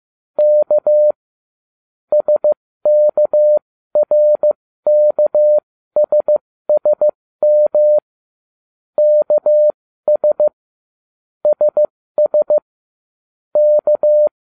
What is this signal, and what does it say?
Morse code, heard as a keyed tone, about 15 wpm, spaced as for 13 wpm, 600 Hz. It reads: K SKRKSSM KS SS K